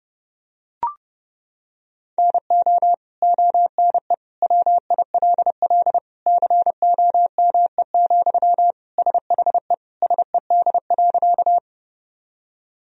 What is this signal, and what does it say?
Morse code, heard as a keyed tone, about 30 wpm, 700 Hz. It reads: NO ONE WILL COME, SHE SED.
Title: Sentences from Top 100 Words and Abbreviations 30wpm